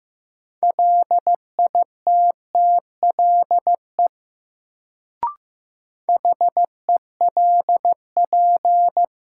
Morse code, heard as a keyed tone, about 15 wpm, 700 Hz. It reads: LITTLE HELP
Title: Sentences from Top 200 Words - New Word Drill - 15wpm